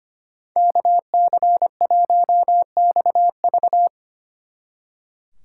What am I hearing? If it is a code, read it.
Morse code, heard as a keyed tone, about 32 wpm, 700 Hz. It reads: KC1XV